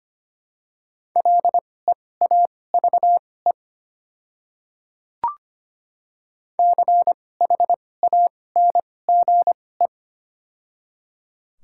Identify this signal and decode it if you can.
Morse code, heard as a keyed tone, about 25 wpm, spaced as for 18 wpm, 700 Hz. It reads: LEAVE CHANGE
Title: 2x Character Spacing - Top 500 Words - 25wpm